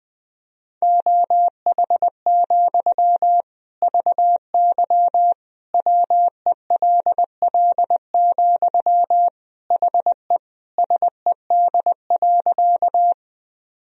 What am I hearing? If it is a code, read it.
Morse code, heard as a keyed tone, about 20 wpm, 700 Hz. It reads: OH, VY WELL, HE SED.
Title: Sentences from Top 500 Words and Abbreviations 20wpm